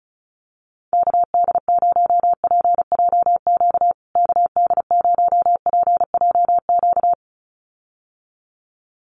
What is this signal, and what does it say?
Morse code, heard as a keyed tone, about 34 wpm, 700 Hz. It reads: KD0PJQ KD0PJQ
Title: USA Call Signs Repeated 2x 35wpm